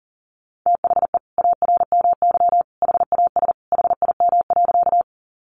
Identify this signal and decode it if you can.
Morse code, heard as a keyed tone, about 40 wpm, 700 Hz. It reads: THE ARMY HAS HIM.
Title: Sentences from Top 800 Words 40wpm